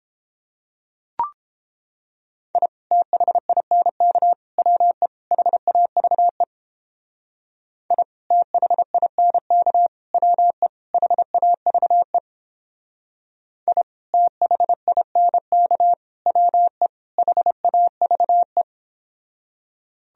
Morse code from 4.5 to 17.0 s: WE HAVE I THINK WE HAVE I THINK WE

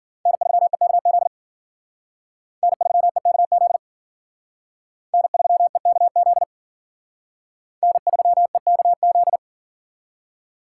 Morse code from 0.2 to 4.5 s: N3EKZ N3EKZ

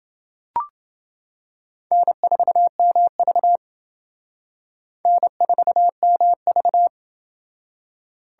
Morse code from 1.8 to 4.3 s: N4MV

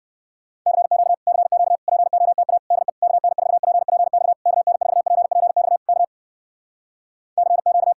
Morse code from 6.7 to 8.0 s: CQ

